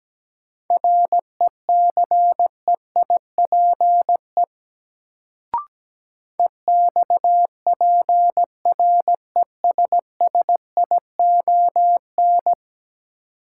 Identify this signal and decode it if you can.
Morse code, heard as a keyed tone, about 17 wpm, 700 Hz. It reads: RECEIPE EXPRESSION
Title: Sentences from Top 1200 Words - New Word Drill - 17wpm